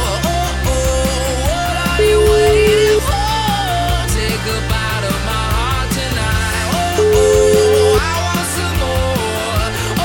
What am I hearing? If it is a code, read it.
Ringback tone (Europe, 425 Hz), with noise.